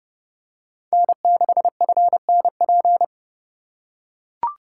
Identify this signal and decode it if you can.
Morse code, heard as a keyed tone, about 28 wpm, 700 Hz. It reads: N6FNP